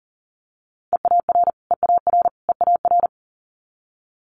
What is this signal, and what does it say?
Morse code, heard as a keyed tone, about 41 wpm, 700 Hz. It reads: EAR EAR EAR
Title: Multi-letter Phonemes Repeated 3x 40wpm